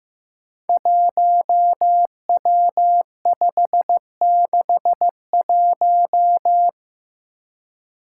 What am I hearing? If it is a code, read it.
Morse code, heard as a keyed tone, about 15 wpm, 700 Hz. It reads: 1W561
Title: Five Challenging Letter-Number Combinations 15wpm